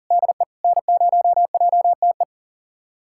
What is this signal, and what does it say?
Morse code, heard as a keyed tone, about 37 wpm, 700 Hz. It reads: DE N0JTE
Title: USA Callsign DE Callsign - 5 Characters - 40wpm